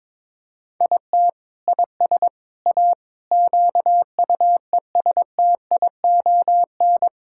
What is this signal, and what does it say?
Morse code, heard as a keyed tone, about 22 wpm, 700 Hz. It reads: IT IS A QUESTION
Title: Sentences from Top 1000 Words - Review Part 6 - 22wpm